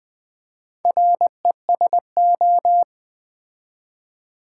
Morse code, heard as a keyed tone, about 20 wpm, 700 Hz. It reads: RESO
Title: Build A Word - 4 Letters At A Time - 8-Letter Words - 20wpm